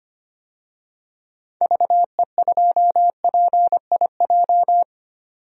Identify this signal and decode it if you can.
Morse code, heard as a keyed tone, about 25 wpm, 700 Hz. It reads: VE2PIJ
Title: DXCC-100 Call Signs - 6 Characters - 25wpm